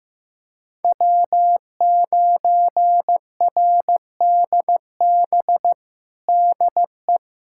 Morse code, heard as a keyed tone, about 15 wpm, 700 Hz. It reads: W9RDB DE
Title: USA Callsign DE Callsign - 5 Characters - 15wpm